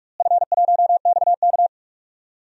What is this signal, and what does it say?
Morse code, heard as a keyed tone, about 42 wpm, 700 Hz. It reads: F1XK
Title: DXCC-100 Call Signs - 4 Characters - Repeated 2x - 45wpm